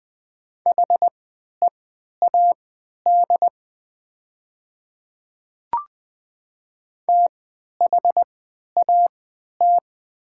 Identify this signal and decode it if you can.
Morse code, heard as a keyed tone, about 20 wpm, spaced as for 11 wpm, 690 Hz. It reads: HEAD THAT
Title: 3x Character Spacing - Top 300 Words - 20wpm